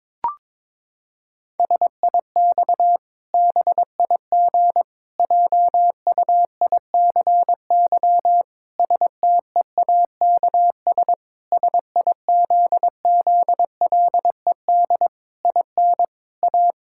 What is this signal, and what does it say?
Morse code, heard as a keyed tone, about 22 wpm, 700 Hz. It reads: SIX BIG JUICY STEAKS SIZZLED IN A
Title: Pangram Challenge v2 - 22wpm